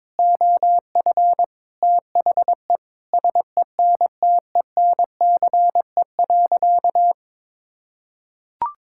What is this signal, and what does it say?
Morse code, heard as a keyed tone, about 22 wpm, 700 Hz. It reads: OF THE SENTENCE.
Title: Sentences from Top 1000 Words - Review Part 6 - 22wpm